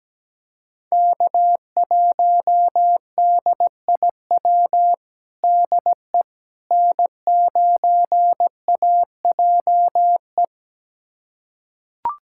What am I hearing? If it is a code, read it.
Morse code, heard as a keyed tone, about 17 wpm, 700 Hz. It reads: K1DIW DE N9AJE
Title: USA Callsign DE Callsign - 5 Characters - 17wpm